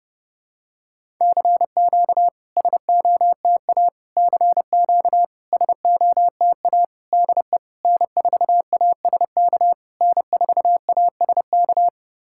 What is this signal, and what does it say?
Morse code, heard as a keyed tone, about 28 wpm, 660 Hz. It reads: CQ SOTA CQ SOTA DE N4ASK N4ASK
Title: SOTA QSO Practice 30wpm